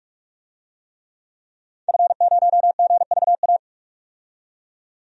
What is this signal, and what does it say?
Morse code, heard as a keyed tone, about 46 wpm, 700 Hz. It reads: F0GVA
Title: DXCC-100 Call Signs - 5 Characters - 45wpm